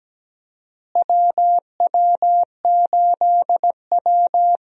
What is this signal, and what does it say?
Morse code, heard as a keyed tone, about 17 wpm, 700 Hz. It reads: WW8W
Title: USA Call Signs - 4 Characters - Repeated 3x - 17wpm